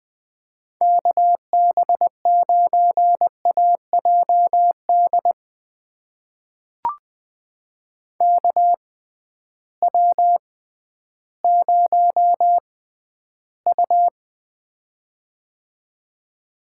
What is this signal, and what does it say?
Morse code, heard as a keyed tone, about 20 wpm, spaced as for 12 wpm, 700 Hz. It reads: KB9AJD KW0U
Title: USA Call Signs - 6x Character Spacing - 20wpm